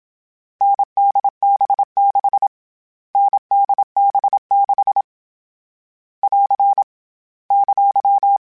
Morse code, heard as a keyed tone, about 27 wpm, 800 Hz. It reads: NDB6 NDB6 <AR> !
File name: Variable Speed / Variable Speed-mp3s / GZ GZ7 sp @25